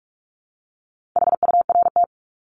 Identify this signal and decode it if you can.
Morse code, heard as a keyed tone, about 44 wpm, 700 Hz. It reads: HURT
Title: Top 1751-2000 Words 45wpm